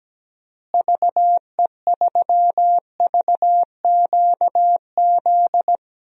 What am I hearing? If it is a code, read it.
Morse code, heard as a keyed tone, about 17 wpm, 700 Hz. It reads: VE3VQZ
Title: DXCC-100 Call Signs - 6 Characters - Repeated 2x - 17wpm